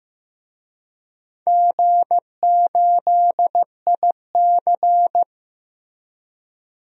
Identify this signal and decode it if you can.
Morse code, heard as a keyed tone, about 15 wpm, 700 Hz. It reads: G8IC
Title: DXCC-100 Call Signs - 4 Characters - Repeated 2x - 15wpm